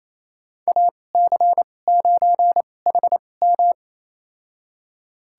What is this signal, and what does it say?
Morse code, heard as a keyed tone, about 28 wpm, spaced as for 20 wpm, 700 Hz. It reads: AC9HM
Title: USA Call Signs - 5 Characters - 2x Character Spacing - 28wpm